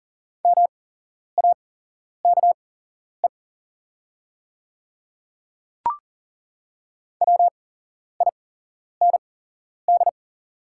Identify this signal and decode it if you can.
Morse code, heard as a keyed tone, about 41 wpm, spaced as for 11 wpm, 690 Hz. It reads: MAKE WIND
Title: 8x Character Spacing - Top 500 Words - 40wpm